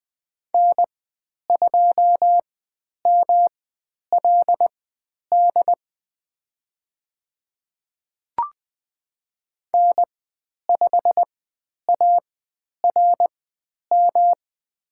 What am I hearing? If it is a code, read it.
Morse code, heard as a keyed tone, about 20 wpm, spaced as for 10 wpm, 700 Hz. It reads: N2MLD N5ARM
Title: 109 Lesson 18g - Letter m - Call Signs 20wpm